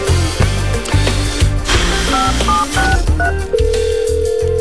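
DTMF, heard as telephone touch tones, tones of about 180 ms. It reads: 5763
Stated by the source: Broadcast on 99.600 MHz FM by COPE radio.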